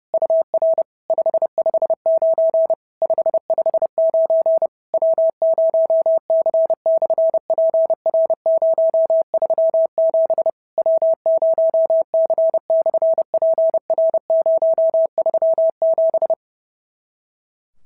Morse code, heard as a keyed tone, about 30 wpm, 650 Hz. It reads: UR 559 559 W0C/PR037 W0C/PR037
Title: SOTA QSO Practice 30wpm